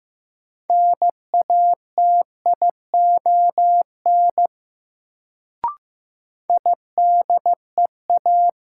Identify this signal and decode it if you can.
Morse code, heard as a keyed tone, about 15 wpm, 700 Hz. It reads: NATION IDEA
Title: Sets of 3 Words - New Word Drill - 15wpm